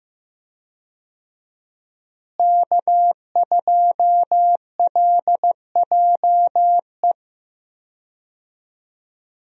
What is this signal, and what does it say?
Morse code, heard as a keyed tone, about 15 wpm, 700 Hz. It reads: K2LJE
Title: USA Call Signs - 5 Characters - Repeated 3x - 15wpm